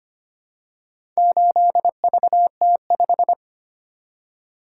Morse code, heard as a keyed tone, about 25 wpm, 700 Hz. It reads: 8VT5
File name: practice / 25wpm / Four Letter-Number Combinations - 25wpm